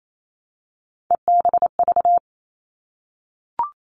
Morse code, heard as a keyed tone, about 28 wpm, 700 Hz. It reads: EBV
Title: All Cut Numbers - 3-Digit 28wpm